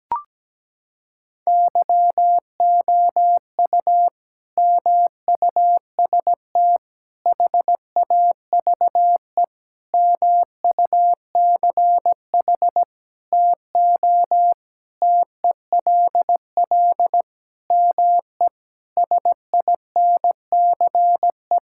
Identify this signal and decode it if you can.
Morse code, heard as a keyed tone, about 17 wpm, 700 Hz. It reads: YOU MUST HAVE MUCH TO TELL ME SINCE
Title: Sentences from Top 1600 Words 17wpm